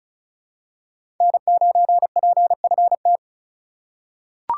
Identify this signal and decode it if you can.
Morse code, heard as a keyed tone, about 34 wpm, 700 Hz. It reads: N9PFT
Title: USA Call Signs - 5 Characters - 2x Character Spacing - 35wpm